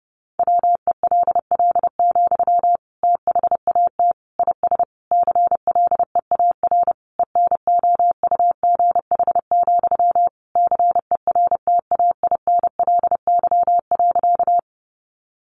Morse code, heard as a keyed tone, about 30 wpm, 700 Hz. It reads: WELL, THAT IS CLEAR ENOUGH, CERTAINLY.
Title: Sentences from Top 900 Words 30wpm